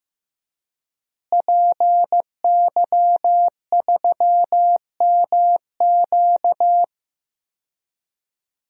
Morse code, heard as a keyed tone, about 15 wpm, 700 Hz. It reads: PY3MQ
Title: DXCC-100 Call Signs - 5 Characters - Repeated 2x - 15wpm